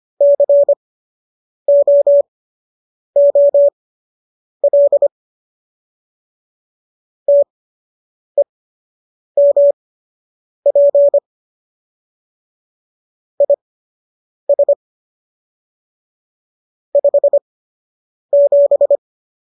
Morse code, heard as a keyed tone, about 25 wpm, spaced as for 8 wpm, 570 Hz. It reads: COOL TEMP IS 57